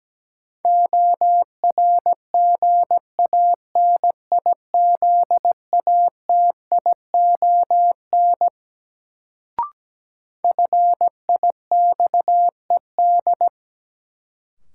Morse code, heard as a keyed tone, about 17 wpm, 700 Hz. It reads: ORGANIZATION FIXED
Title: Sentences from Top 1100 Words - New Word Drill Alt - 17wpm